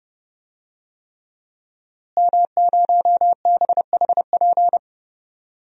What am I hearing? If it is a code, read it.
Morse code, heard as a keyed tone, about 30 wpm, 700 Hz. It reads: M0BHP
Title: DXCC-100 Call Signs - 5 Characters - Repeated 3x - 30wpm